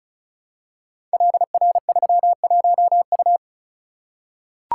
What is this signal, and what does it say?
Morse code, heard as a keyed tone, about 36 wpm, 700 Hz. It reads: LR31U
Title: Code Groups - 5 Letter:Numbers - 35wpm